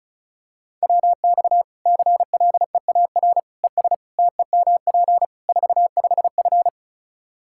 Morse code, heard as a keyed tone, about 34 wpm, 700 Hz. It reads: WX CLEAR ES TEMP 45F
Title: Practice QSO - Exchange 2 - Info - 35wpm